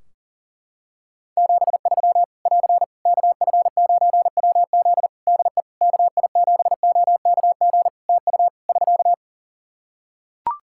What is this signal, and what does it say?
Morse code, heard as a keyed tone, about 40 wpm, 700 Hz. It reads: 73 <AR> KF9WZ DE KI7OKG TU <SK>
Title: K1USN SST QSO Practice 40wpm